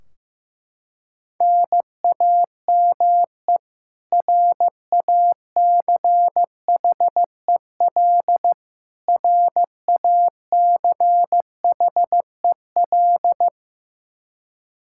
Morse code, heard as a keyed tone, about 15 wpm, 700 Hz. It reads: NAME RACHEL RACHEL
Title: Practice QSO - Exchange 1 - Beginning - 15wpm